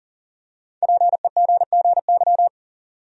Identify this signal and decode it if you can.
Morse code, heard as a keyed tone, about 39 wpm, 700 Hz. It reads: PEGGY
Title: USA Common Names - 40wpm